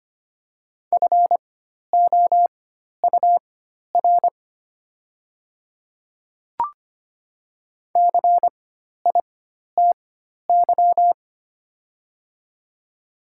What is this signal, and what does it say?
Morse code, heard as a keyed tone, about 25 wpm, spaced as for 12 wpm, 700 Hz. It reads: FOUR CITY
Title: 4x Character Spacing - Top 300 Words - 25wpm